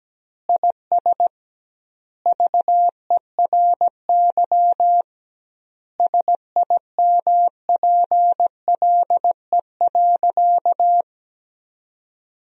Morse code, heard as a keyed tone, about 17 wpm, 700 Hz. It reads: IS VERY SIMPLE.
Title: 2x Word-Spacing - Sentences from Top 500 Words - 17wpm